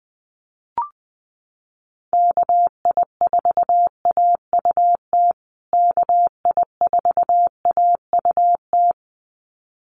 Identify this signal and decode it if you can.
Morse code, heard as a keyed tone, about 20 wpm, 700 Hz. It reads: KI4AUT KI4AUT
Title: USA Call Signs Repeated 2x 20wpm